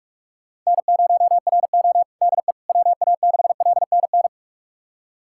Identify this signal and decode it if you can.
Morse code, heard as a keyed tone, about 46 wpm, 700 Hz. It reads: N0RO DE WA6LNN
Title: USA Callsign DE Callsign 45wpm